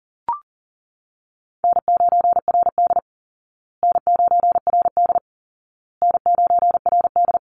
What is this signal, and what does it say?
Morse code, heard as a keyed tone, about 40 wpm, 700 Hz. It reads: N9RD N9RD N9RD
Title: Call Signs Repeated 2x - 40wpm